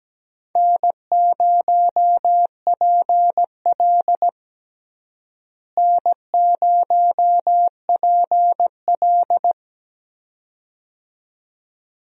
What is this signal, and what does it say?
Morse code, heard as a keyed tone, about 17 wpm, 700 Hz. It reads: N0PL N0PL
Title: USA Call Signs - 4 Characters - Repeated 2x - 17wpm